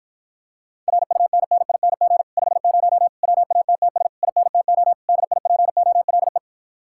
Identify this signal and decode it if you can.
Morse code, heard as a keyed tone, about 53 wpm, 700 Hz. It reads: RUNNING 50 WATTS INTO DIPOLE